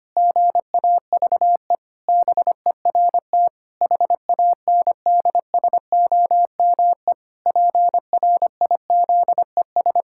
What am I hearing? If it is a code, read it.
Morse code, heard as a keyed tone, about 25 wpm, 700 Hz. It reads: GAVE BERT HANDSOME PRIZES
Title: Pangram Challenge v2 - 25wpm